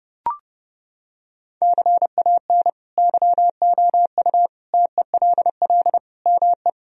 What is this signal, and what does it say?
Morse code, heard as a keyed tone, about 30 wpm, 700 Hz. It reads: CAN YOU TELL ME
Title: Sentences from Top 1000 Words - Review Part 2 - 30wpm